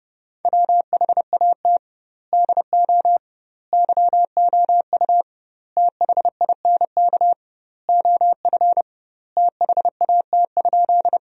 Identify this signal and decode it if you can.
Morse code, heard as a keyed tone, about 30 wpm, 700 Hz. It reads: WHAT DO YOU THINK OF THAT?
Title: 2x Word-Spacing - Sentences from Top 200 Words - 30wpm